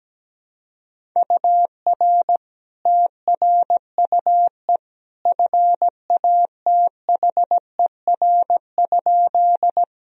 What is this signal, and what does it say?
Morse code, heard as a keyed tone, about 17 wpm, 700 Hz. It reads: UR TRUE FATHER?
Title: Sentences from Top 400 Words and Abbreviations 17wpm